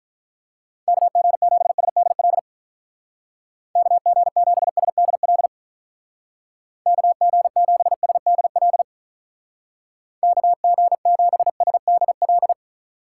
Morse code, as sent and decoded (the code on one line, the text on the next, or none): -.- --. --... ... -.. .-.. / -.- --. --... ... -.. .-.. / -.- --. --... ... -.. .-.. / -.- --. --... ... -.. .-..
KG7SDL KG7SDL KG7SDL KG7SDL